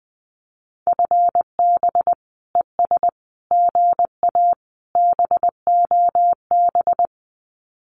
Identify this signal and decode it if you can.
Morse code, heard as a keyed tone, about 19 wpm, 700 Hz. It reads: FB ES GA BOB
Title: Common QSO Phrases Encore 20wpm